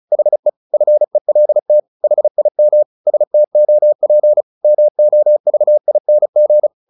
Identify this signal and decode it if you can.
Morse code, heard as a keyed tone, about 34 wpm, 600 Hz. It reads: HE FELT HIM STOP MOVING
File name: the old man and the sea / the old man and the sea 1 35wpm